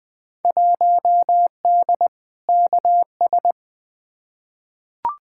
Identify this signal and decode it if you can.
Morse code, heard as a keyed tone, about 20 wpm, 700 Hz. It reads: 1D KS
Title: Field Day Exchanges - 20wpm